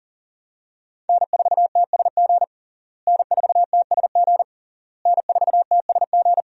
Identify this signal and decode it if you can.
Morse code, heard as a keyed tone, about 40 wpm, 700 Hz. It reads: N4TSG N4TSG N4TSG
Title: USA Call Signs - 5 Characters - Repeated 3x - 40wpm